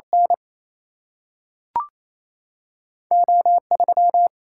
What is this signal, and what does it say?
Morse code, heard as a keyed tone, about 28 wpm, 700 Hz. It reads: N O3
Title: Single-Letter Single-Number 28wpm